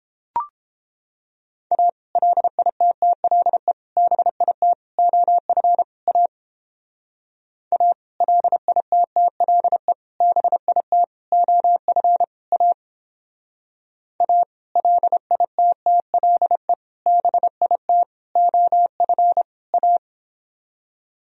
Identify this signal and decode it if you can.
Morse code, heard as a keyed tone, about 31 wpm, 700 Hz. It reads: A LITTLE BIT OF A A LITTLE BIT OF A A LITTLE BIT OF A